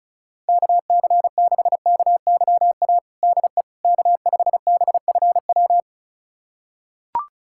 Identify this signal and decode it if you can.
Morse code, heard as a keyed tone, about 34 wpm, 690 Hz. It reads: KC6KYA DE K5BFW
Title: USA Callsign DE Callsign 35wpm